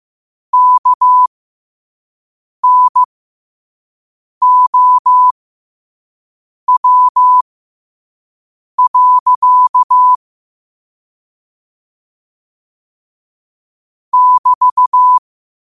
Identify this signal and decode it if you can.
Morse code, heard as a keyed tone, about 15 wpm, spaced as for 5 wpm, 1000 Hz. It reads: KNOW. <BT>
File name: qotdfiles / 5wpm